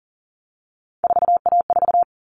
Morse code, heard as a keyed tone, about 40 wpm, 700 Hz. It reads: 4A4
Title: All Cut Numbers - 3-Digit 40wpm